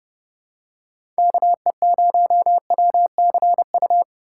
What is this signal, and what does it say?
Morse code, heard as a keyed tone, about 30 wpm, 700 Hz. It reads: KE0WCU